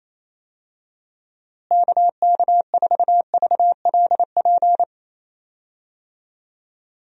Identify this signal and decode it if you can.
Morse code, heard as a keyed tone, about 28 wpm, 700 Hz. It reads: KK4VLP
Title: USA Call Signs - 6 Characters - Repeated 3x - 28wpm